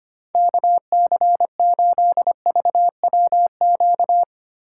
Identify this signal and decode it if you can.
Morse code, heard as a keyed tone, about 31 wpm, 700 Hz. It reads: KC8VWQ
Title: Speed Racing - USA 2x3 Call Signs 25wpm